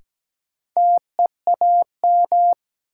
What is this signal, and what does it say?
Morse code, heard as a keyed tone, about 17 wpm, 700 Hz. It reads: TEAM
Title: Sentences from Top 1500 Words - New Word Drill Alt - 17wpm